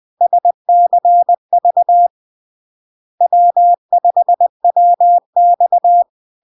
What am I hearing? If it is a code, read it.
Morse code, heard as a keyed tone, about 20 wpm, 700 Hz. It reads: SCV W5WX
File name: LARK-FD-2022-Log-in-CW-20wpm-700-Hz